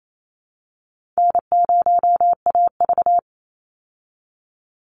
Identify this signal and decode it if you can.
Morse code, heard as a keyed tone, about 28 wpm, 700 Hz. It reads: N0AV
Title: Easy Call Signs Repeated 2x - 28wpm